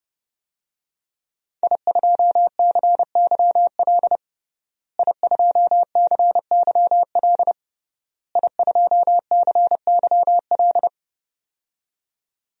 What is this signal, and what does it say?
Morse code, heard as a keyed tone, about 32 wpm, 700 Hz. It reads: I2CYL I2CYL I2CYL
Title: DXCC-100 Call Signs - 5 Characters - Repeated 3x - 30wpm